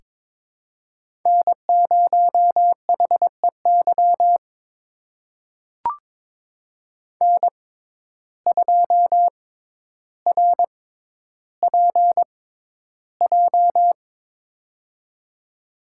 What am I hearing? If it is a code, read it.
Morse code, heard as a keyed tone, about 22 wpm, spaced as for 11 wpm, 700 Hz. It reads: N0HEY N2RPJ
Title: USA Call Signs - 5 Characters - 6x Character Spacing - 22wpm